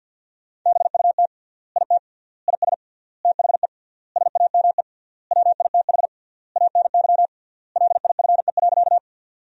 Morse code, heard as a keyed tone, about 51 wpm, 700 Hz. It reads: BUT IT IS THE SAME WITH ANY LIFE.
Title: 3x Word-Spacing - Sentences from Top 300 Words - 50wpm